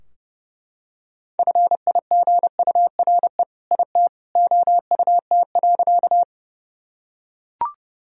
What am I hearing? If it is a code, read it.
Morse code, heard as a keyed tone, about 30 wpm, 700 Hz. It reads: FIGURE IT OUT.
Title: Sentences from Top 1000 Words - Review Part 2 - 30wpm